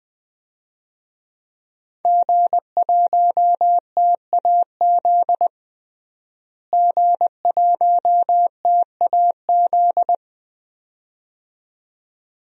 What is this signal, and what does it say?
Morse code, heard as a keyed tone, about 20 wpm, 700 Hz. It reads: G1TAZ G1TAZ
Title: DXCC-100 Call Signs - 5 Characters - Repeated 2x - 20wpm